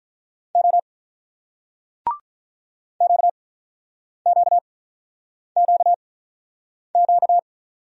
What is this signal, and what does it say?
Morse code, heard as a keyed tone, about 44 wpm, spaced as for 9 wpm, 700 Hz. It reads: O QQQQ